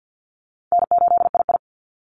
Morse code, heard as a keyed tone, about 48 wpm, 700 Hz. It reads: N8II
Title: Call Signs Repeated 3x - 50wpm